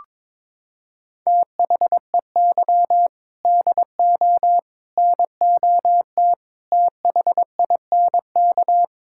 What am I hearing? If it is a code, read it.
Morse code, heard as a keyed tone, about 22 wpm, 700 Hz. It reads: THEY DO NOT THINK
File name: practice / 22wpm / Sentences with 4-Word Phrases - 22wpm